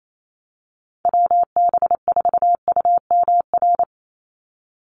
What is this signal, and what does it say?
Morse code, heard as a keyed tone, about 28 wpm, 700 Hz. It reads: WB4UMR
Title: USA Call Signs 28wpm